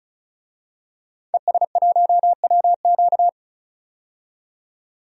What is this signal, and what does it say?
Morse code, heard as a keyed tone, about 34 wpm, 700 Hz. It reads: ES1WQ
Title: DXCC-100 Call Signs - 5 Characters - 35wpm